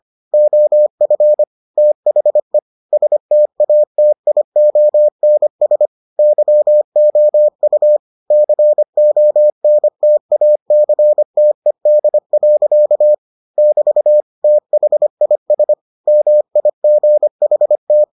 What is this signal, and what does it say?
Morse code, heard as a keyed tone, about 25 wpm, 600 Hz. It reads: OF THE STATIONS YOU CONTACTED. <BT> THIS MIGHT